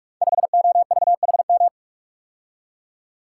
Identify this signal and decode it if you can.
Morse code, heard as a keyed tone, about 41 wpm, 700 Hz. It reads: 5OVHM
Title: Code Groups - 5 Letter:Numbers - 45wpm